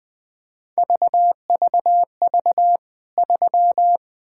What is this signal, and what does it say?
Morse code, heard as a keyed tone, about 20 wpm, 700 Hz. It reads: VVV 3
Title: K1USN SST QSO Practice 20wpm